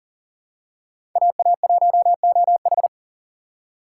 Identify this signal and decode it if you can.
Morse code, heard as a keyed tone, about 37 wpm, 700 Hz. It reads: AA1OH
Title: USA Call Signs - 5 Characters - 4x Character Spacing - 40wpm